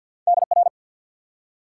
Morse code, heard as a keyed tone, about 48 wpm, 700 Hz. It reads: DR
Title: Top 90 QSO Elements 50wpm